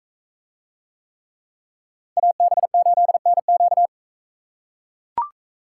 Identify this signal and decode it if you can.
Morse code, heard as a keyed tone, about 39 wpm, 700 Hz. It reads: AB8NQ